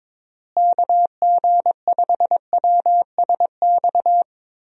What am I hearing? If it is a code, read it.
Morse code, heard as a keyed tone, about 22 wpm, 700 Hz. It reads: KG5WSX
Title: USA Call Signs - 6 Characters - Repeated 3x - 22wpm